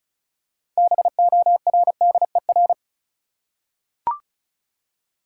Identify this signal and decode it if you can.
Morse code, heard as a keyed tone, about 34 wpm, 700 Hz. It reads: BORDER
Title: Six Letter Words - 35wpm